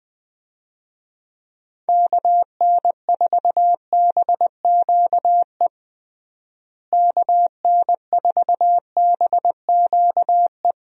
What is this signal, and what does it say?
Morse code, heard as a keyed tone, about 20 wpm, 700 Hz. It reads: KN4BQE KN4BQE
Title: USA Call Signs - 6 Characters - Repeated 2x - 20wpm